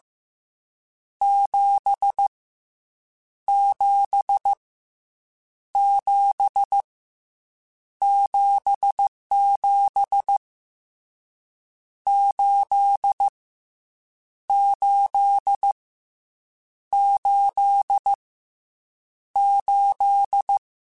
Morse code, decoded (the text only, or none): 7 7 7 77 8 8 8 8